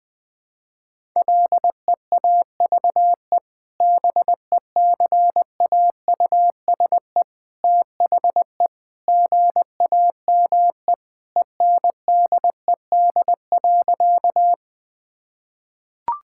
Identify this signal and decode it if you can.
Morse code, heard as a keyed tone, about 20 wpm, 700 Hz. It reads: LEAVE BECAUSE THE GAME ENDED.
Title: Sentences with 2-Word Phrases - 20wpm